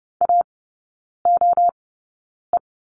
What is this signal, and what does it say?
Morse code, heard as a keyed tone, about 30 wpm, spaced as for 9 wpm, 700 Hz. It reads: AOE
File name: course / 30:10wpm / 087 Lesson 15e - Number 5 - Three Characters 30wpm